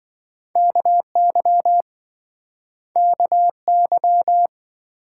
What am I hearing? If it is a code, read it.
Morse code, heard as a keyed tone, about 27 wpm, 700 Hz. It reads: KY KY